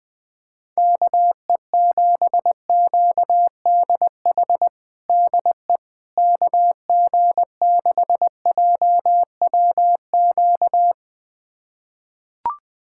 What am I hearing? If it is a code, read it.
Morse code, heard as a keyed tone, about 20 wpm, 700 Hz. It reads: KE7QDH DE KG6JWQ
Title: USA Callsign DE Callsign - 6 Characters - 20wpm